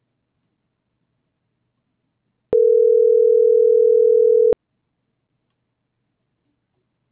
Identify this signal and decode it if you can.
Ringback tone (North America), 440 + 480 Hz.